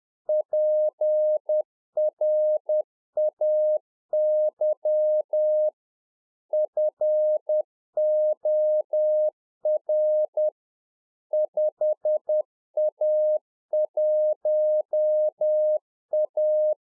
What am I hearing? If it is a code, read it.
Morse code, heard as a keyed tone, about 10 wpm, 610 Hz. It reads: PRAY FOR 5A1A
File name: UNFOGETTABLE QSO 10WPM